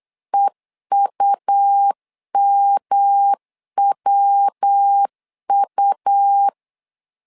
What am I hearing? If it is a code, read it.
Morse code, heard as a keyed tone, about 8 wpm, 800 Hz. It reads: EUMWU
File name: LEZIONI / 3 MWRO / 3 MW 8wpm 2501